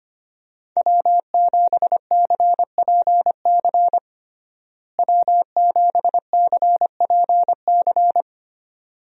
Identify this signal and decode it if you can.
Morse code, heard as a keyed tone, about 25 wpm, 700 Hz. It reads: W7CPC W7CPC
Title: USA Call Signs - 5 Characters - Repeated 2x - 25wpm